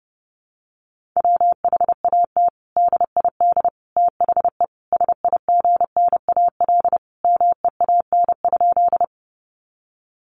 Morse code, heard as a keyed tone, about 30 wpm, 700 Hz. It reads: WHAT DID THE SIGNAL MEAN?
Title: Sentences from Top 1400 Words 30wpm